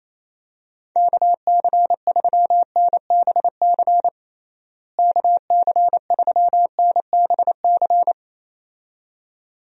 Morse code, heard as a keyed tone, about 27 wpm, 700 Hz. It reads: KC3NBC KC3NBC
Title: USA Call Signs - 6 Characters - Repeated 2x - 28wpm